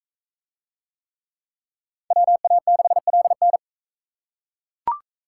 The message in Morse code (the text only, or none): WA6LN